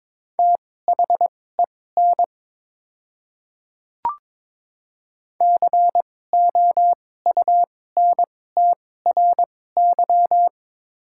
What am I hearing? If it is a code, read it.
Morse code, heard as a keyed tone, about 22 wpm, spaced as for 16 wpm, 700 Hz. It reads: THEN COUNTRY
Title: 2x Character Spacing - Top 300 Words - 22wpm